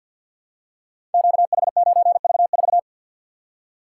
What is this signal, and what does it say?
Morse code, heard as a keyed tone, about 46 wpm, 700 Hz. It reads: QH9V4
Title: Code Groups - 5 Letter:Numbers - 50wpm